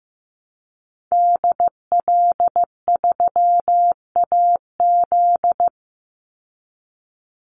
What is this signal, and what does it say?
Morse code, heard as a keyed tone, about 15 wpm, 700 Hz. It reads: DL3AZ
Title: Call Signs Repeated 3x - 15wpm